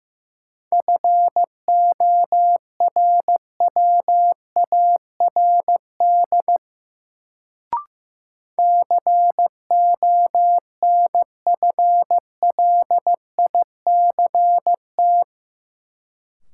Morse code, heard as a keyed tone, about 15 wpm, 700 Hz. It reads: FORWARD CONFLICT
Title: Sentences from Top 1300 Words - New Word Drill - 15wpm